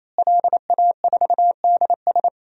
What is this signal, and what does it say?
Morse code, heard as a keyed tone, about 28 wpm, 700 Hz. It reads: LA4DS
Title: DXCC-100 Call Signs - 5 Characters - Repeated 3x - 28wpm